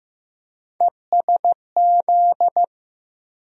Morse code, heard as a keyed tone, about 15 wpm, 700 Hz. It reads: ESZ
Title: USA Call Sign Postfixes - 3 Letters 15wpm